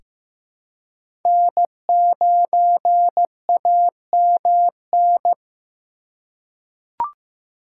Morse code, heard as a keyed tone, about 15 wpm, 700 Hz. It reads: N9AMN
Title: USA Call Signs - 5 Characters - 2x Character Spacing - 15wpm